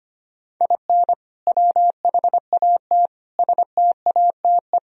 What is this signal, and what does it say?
Morse code, heard as a keyed tone, about 25 wpm, 700 Hz. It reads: IN WHAT STATE
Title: Sentences from Top 1000 Words - Review Part 2 - 25wpm